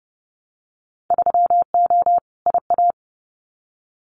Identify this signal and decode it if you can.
Morse code, heard as a keyed tone, about 30 wpm, 700 Hz. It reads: 3O IA
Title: Winter Field Day Exchanges 30wpm